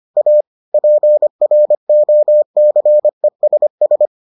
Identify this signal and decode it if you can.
Morse code, heard as a keyed tone, about 25 wpm, 600 Hz. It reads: A PROCESS